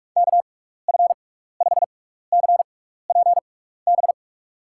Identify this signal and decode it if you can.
Morse code, heard as a keyed tone, about 46 wpm, spaced as for 16 wpm, 700 Hz. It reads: KF5CPB
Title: USA Call Signs - 6x Character Spacing - 45wpm